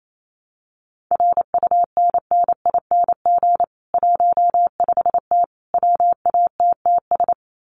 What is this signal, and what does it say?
Morse code, heard as a keyed tone, about 28 wpm, 700 Hz. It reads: RUNNING 15T WATTS
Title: Common QSO Phrases 28wpm